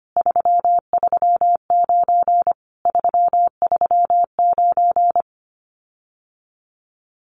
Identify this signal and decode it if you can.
Morse code, heard as a keyed tone, about 25 wpm, 700 Hz. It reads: 339 339
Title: Common QSO Phrases Encore 25wpm